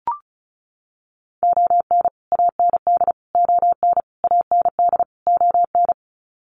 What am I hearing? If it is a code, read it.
Morse code, heard as a keyed tone, about 34 wpm, 700 Hz. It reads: ON AND ON AND ON
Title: Sets of 5 Words Encore2 35wpm